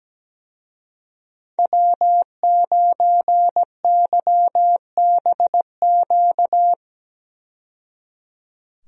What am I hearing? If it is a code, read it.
Morse code, heard as a keyed tone, about 17 wpm, 700 Hz. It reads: W9YBQ
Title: USA Call Signs - 5 Characters - 17wpm